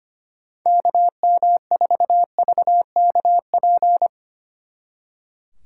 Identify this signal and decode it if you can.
Morse code, heard as a keyed tone, about 31 wpm, 700 Hz. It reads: KM4VKP